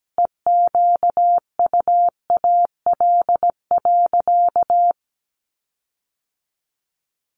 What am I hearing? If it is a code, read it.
Morse code, heard as a keyed tone, about 17 wpm, 700 Hz. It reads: EQUAL.
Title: Sentences from Top 1500 Words 17wpm